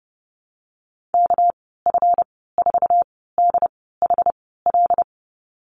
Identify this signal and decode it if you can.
Morse code, heard as a keyed tone, about 30 wpm, spaced as for 17 wpm, 700 Hz. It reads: KF4DHL
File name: Lesson 37g - Letter x - Call Signs 30wpm